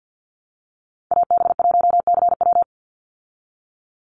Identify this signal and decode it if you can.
Morse code, heard as a keyed tone, about 50 wpm, 700 Hz. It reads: AB1CW
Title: Call Signs Repeated 2x - 50wpm